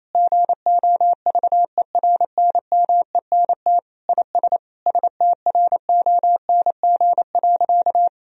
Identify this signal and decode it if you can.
Morse code, heard as a keyed tone, about 28 wpm, 700 Hz. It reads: GOVERNMENT IS STRONG.
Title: Sentences from Top 1000 Words - Review Part 4 - 28wpm